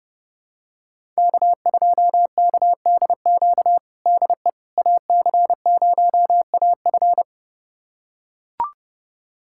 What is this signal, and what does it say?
Morse code, heard as a keyed tone, about 30 wpm, 700 Hz. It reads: K2KDQ DE AC0AF
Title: USA Callsign DE Callsign - 5 Characters - 30wpm